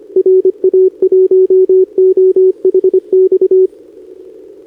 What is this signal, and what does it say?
Morse code, heard as a keyed tone, about 23 wpm, 410 Hz. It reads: RA1OHX
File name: CQWWCW2025 / NedeleOdpol 20m Scan